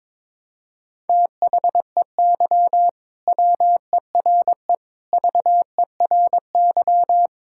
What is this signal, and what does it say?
Morse code, heard as a keyed tone, about 21 wpm, 700 Hz. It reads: THEY WERE VERY
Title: Sentences from Top 1000 Words - Review Part 5 - 22wpm